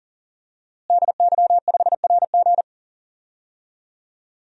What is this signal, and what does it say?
Morse code, heard as a keyed tone, about 40 wpm, 700 Hz. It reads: DY5RG
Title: DXCC-100 Call Signs - 5 Characters - Repeated 2x - 40wpm